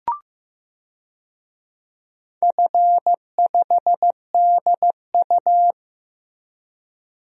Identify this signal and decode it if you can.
Morse code, heard as a keyed tone, about 16 wpm, 700 Hz. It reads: F5DU
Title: DXCC-100 Call Signs - 4 Characters - Repeated 2x - 15wpm